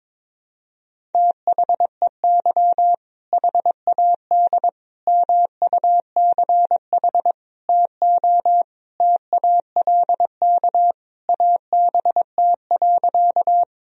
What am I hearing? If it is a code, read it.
Morse code, heard as a keyed tone, about 22 wpm, 700 Hz. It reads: THEY HAD MUCH TO TALK ABT.
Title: Sentences from Top 400 Words and Abbreviations 22wpm